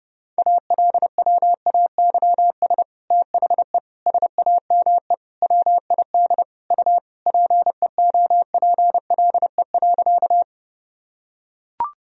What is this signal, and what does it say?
Morse code, heard as a keyed tone, about 30 wpm, 700 Hz. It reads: ALWAYS THE SAME WID U PEOPLE.
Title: Sentences from Top 300 Words and Abbreviations 30wpm